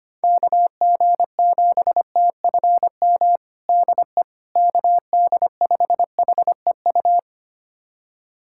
Morse code, heard as a keyed tone, about 25 wpm, 700 Hz. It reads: KG7TFM DE KD5HEU
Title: USA Callsign DE Callsign - 6 Characters - 25wpm